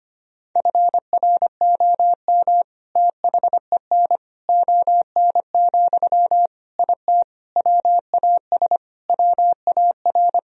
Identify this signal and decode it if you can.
Morse code, heard as a keyed tone, about 25 wpm, 700 Hz. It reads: FROM THEN ON, IT WAS WAR
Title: Sentences from Top 1000 Words - Review Part 2 - 25wpm